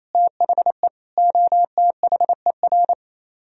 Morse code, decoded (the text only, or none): THE OTHER